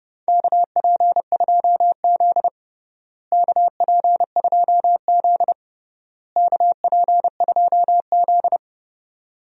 Morse code, heard as a keyed tone, about 32 wpm, 700 Hz. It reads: KP2Z KP2Z KP2Z
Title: DXCC-100 Call Signs - 4 Characters - Repeated 3x - 30wpm